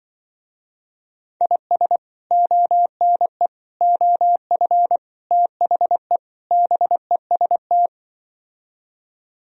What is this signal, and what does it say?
Morse code, heard as a keyed tone, about 25 wpm, 700 Hz. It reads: IS ONE OF THE BEST